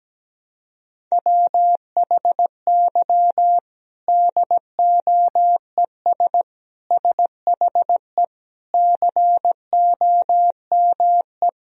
Morse code, heard as a keyed tone, about 17 wpm, 700 Hz. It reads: WHY DOES SHE COME
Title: Sentences from Top 1000 Words - Review Part 1 - 17wpm